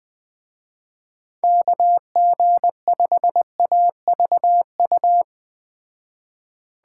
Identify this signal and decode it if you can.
Morse code, heard as a keyed tone, about 19 wpm, 700 Hz. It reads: KG5AVU
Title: Speed Racing - USA 2x3 Call Signs 17wpm